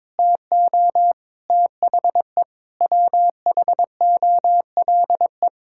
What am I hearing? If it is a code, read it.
Morse code, heard as a keyed tone, about 22 wpm, 700 Hz. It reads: TO THE WHOLE
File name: Sentences with 7-Word Phrases - 22wpm